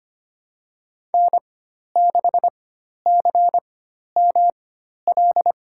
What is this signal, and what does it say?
Morse code, heard as a keyed tone, about 25 wpm, spaced as for 12 wpm, 700 Hz. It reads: N6CML
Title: USA Call Signs - 5 Characters - 4x Character Spacing - 25wpm